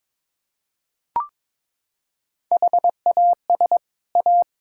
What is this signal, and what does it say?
Morse code, heard as a keyed tone, about 22 wpm, 700 Hz. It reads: HAS A
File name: practice / 22wpm / Sets of 6 Words - 22wpm